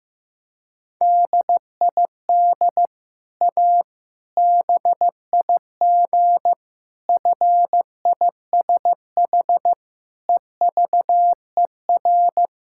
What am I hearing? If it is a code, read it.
Morse code, heard as a keyed tone, about 15 wpm, 700 Hz. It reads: DID A BIG FISH EVER
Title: Sentences from Top 1000 Words - Review Part 6 - 15wpm